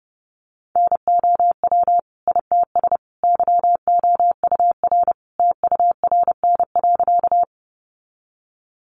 Morse code, heard as a keyed tone, about 30 wpm, 700 Hz. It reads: NOW ITS YOUR TURN.
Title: Sentences from Top 600 Words 30wpm